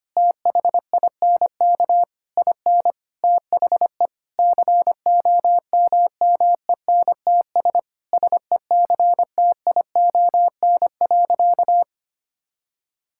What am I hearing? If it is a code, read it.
Morse code, heard as a keyed tone, about 25 wpm, 700 Hz. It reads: THINK IN THE COMMENTS SECTION.
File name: practice / 25wpm / Sentences with 9-Word Phrases - 25wpm